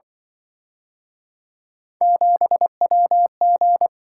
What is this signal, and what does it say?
Morse code, heard as a keyed tone, about 29 wpm, 700 Hz. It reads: 7WG